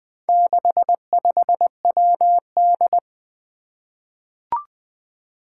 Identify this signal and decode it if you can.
Morse code, heard as a keyed tone, about 20 wpm, 700 Hz. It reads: 65WD
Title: Four Challenging Letter-Number Combinations 20wpm